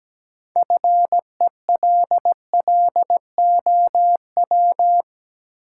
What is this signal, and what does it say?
Morse code, heard as a keyed tone, about 17 wpm, 700 Hz. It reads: FELLOW
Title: Sets of 5 Words Encore - New Word Drill - 17wpm